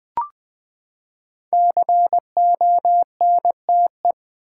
Morse code, heard as a keyed tone, about 20 wpm, 700 Hz. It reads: CONTE
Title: Build A Word - 5 Letters At A Time - 12-Letter Words - 20wpm